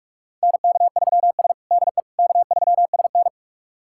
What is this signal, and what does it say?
Morse code, heard as a keyed tone, about 44 wpm, 700 Hz. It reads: NK3S DE K3SN
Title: USA Callsign DE Callsign - 4 Characters - 45wpm